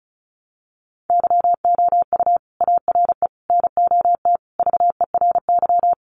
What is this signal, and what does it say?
Morse code, heard as a keyed tone, about 34 wpm, 700 Hz. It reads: YOU ARE NOT VERY
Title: Sentences from Top 1200 Words 35wpm